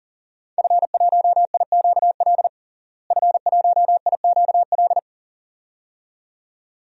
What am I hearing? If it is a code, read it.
Morse code, heard as a keyed tone, about 40 wpm, 700 Hz. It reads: F1IQL F1IQL
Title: DXCC-100 Call Signs - 5 Characters - Repeated 2x - 40wpm